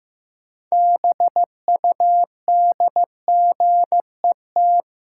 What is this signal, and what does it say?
Morse code, heard as a keyed tone, about 16 wpm, 700 Hz. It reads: BUDGET
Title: Sentences from Top 1200 Words - New Word Drill - 15wpm